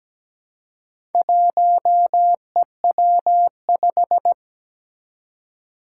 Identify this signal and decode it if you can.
Morse code, heard as a keyed tone, about 17 wpm, 700 Hz. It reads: 1EW5
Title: Four Challenging Letter-Number Combinations 17wpm